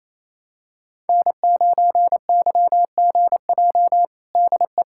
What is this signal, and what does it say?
Morse code, heard as a keyed tone, about 27 wpm, 700 Hz. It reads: N9YGJ DE
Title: USA Callsign DE Callsign 28wpm